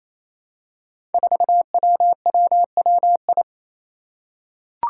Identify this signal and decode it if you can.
Morse code, heard as a keyed tone, about 28 wpm, 700 Hz. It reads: 4WWWI
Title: Five Challenging Letter-Number Combinations 28wpm